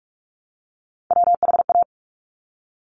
Morse code, heard as a keyed tone, about 46 wpm, 700 Hz. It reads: WHA
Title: Top Three Letter Combinations in Words 45wpm